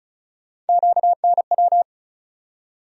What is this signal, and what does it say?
Morse code, heard as a keyed tone, about 36 wpm, 700 Hz. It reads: QNW
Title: USA Call Sign Postfixes - 3 Letters 35wpm